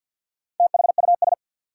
Morse code, heard as a keyed tone, about 48 wpm, 700 Hz. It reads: THUS
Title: Sentences from Top 800 Words - New Word Drill - 50wpm